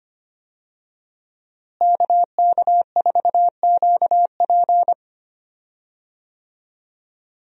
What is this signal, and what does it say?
Morse code, heard as a keyed tone, about 25 wpm, 700 Hz. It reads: KK4QP
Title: USA Call Signs - 5 Characters - Repeated 3x - 25wpm